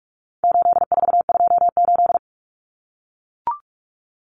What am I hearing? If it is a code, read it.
Morse code, heard as a keyed tone, about 44 wpm, 700 Hz. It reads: 8428
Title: Numbers - 4-Digit 45wpm